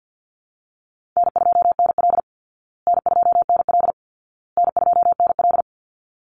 Morse code, heard as a keyed tone, about 50 wpm, 700 Hz. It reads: N2NL N2NL N2NL
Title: Easy Call Signs Repeated 3x - 50wpm